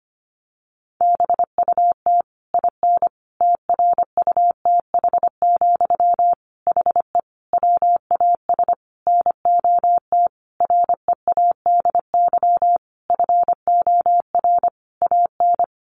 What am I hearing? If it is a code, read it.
Morse code, heard as a keyed tone, about 25 wpm, 700 Hz. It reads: BUT IN TRUTH, HE WAS NOT READY FOR AN
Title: Sentences from Top 1500 Words 25wpm